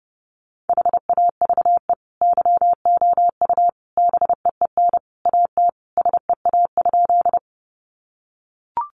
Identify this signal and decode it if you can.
Morse code, heard as a keyed tone, about 30 wpm, 700 Hz. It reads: HAVE YOU BEEN AT SEA?
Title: Sentences from Top 300 Words 30wpm